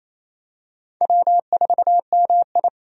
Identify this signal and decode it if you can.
Morse code, heard as a keyed tone, about 28 wpm, 690 Hz. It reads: W4MI
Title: USA Call Signs - 4 Characters - 28wpm